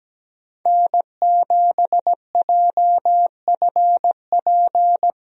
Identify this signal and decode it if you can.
Morse code, heard as a keyed tone, about 17 wpm, 700 Hz. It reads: N7JFP
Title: USA Call Signs - 5 Characters - Repeated 3x - 17wpm